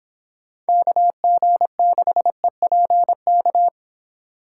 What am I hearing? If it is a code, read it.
Morse code, heard as a keyed tone, about 26 wpm, 700 Hz. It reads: KG6EPK